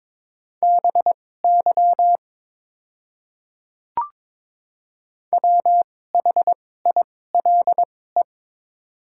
Morse code, heard as a keyed tone, about 22 wpm, spaced as for 16 wpm, 700 Hz. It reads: BY WHILE
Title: 2x Character Spacing - Top 500 Words - 22wpm